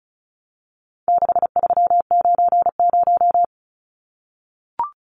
Morse code, heard as a keyed tone, about 34 wpm, 700 Hz. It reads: 6390
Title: Numbers - 4-Digit 35wpm